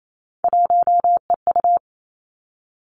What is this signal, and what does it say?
Morse code, heard as a keyed tone, about 28 wpm, 700 Hz. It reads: 1EU
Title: Three Challenging Letter-Number Combinations 28wpm